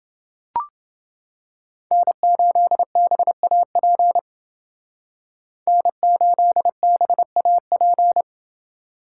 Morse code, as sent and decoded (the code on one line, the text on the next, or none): -. ---.. -... .- .--. / -. ---.. -... .- .--.
N8BAP N8BAP